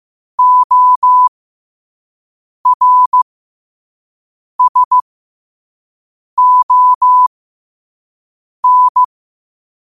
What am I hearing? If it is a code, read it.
Morse code, heard as a keyed tone, about 15 wpm, spaced as for 5 wpm, 1000 Hz. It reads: ORSON
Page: New quotes every day in morse code at 5 Words per minute.